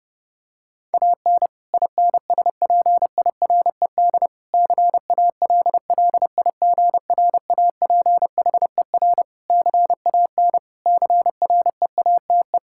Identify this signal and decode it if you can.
Morse code, heard as a keyed tone, about 30 wpm, 700 Hz. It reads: AN INSPIRED CALLIGRAPHER CAN CREATE
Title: Pangram Challenge v2 - 30wpm